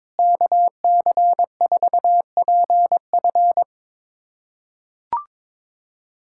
Morse code, heard as a keyed tone, about 22 wpm, 700 Hz. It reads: KC4PF
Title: USA Callsign DE Callsign - 5 Characters - 22wpm